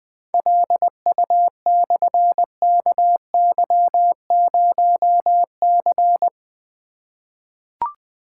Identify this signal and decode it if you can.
Morse code, heard as a keyed tone, about 20 wpm, 700 Hz. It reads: LU/KY0C
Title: Difficult Call Signs 20wpm